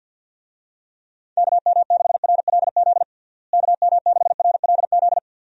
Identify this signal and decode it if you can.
Morse code, heard as a keyed tone, about 50 wpm, 700 Hz. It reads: KM6RLZ KM6RLZ
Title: USA Call Signs - 6 Characters - Repeated 3x - 50wpm